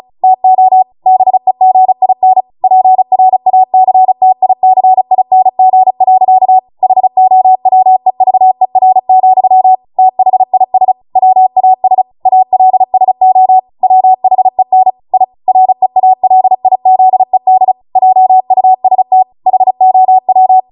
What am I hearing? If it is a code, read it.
Morse code, heard as a keyed tone, about 36 wpm, 750 Hz. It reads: TO BEGIN PRACTICING. HOWEVER, THIS WAS ALSO WHEN I REALIZED JUST HOW